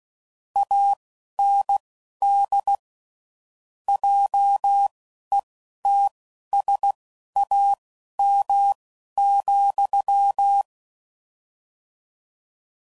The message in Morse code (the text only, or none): AND JETSAM,